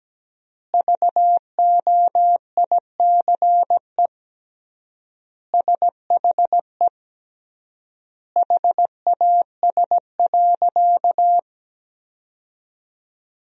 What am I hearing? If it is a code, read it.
Morse code, heard as a keyed tone, about 17 wpm, 700 Hz. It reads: VOICE SHE HAS.
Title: 3x Word-Spacing - Sentences from Top 500 Words - 17wpm